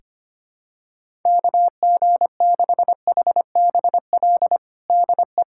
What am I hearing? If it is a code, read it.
Morse code, heard as a keyed tone, about 25 wpm, 700 Hz. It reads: KG6HBL DE
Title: Practice QSO - Exchange 4 - Ending - 25wpm